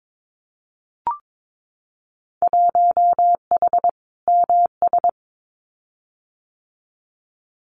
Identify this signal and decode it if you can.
Morse code, heard as a keyed tone, about 22 wpm, 700 Hz. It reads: 1H MS
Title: Winter Field Day Exchanges 22wpm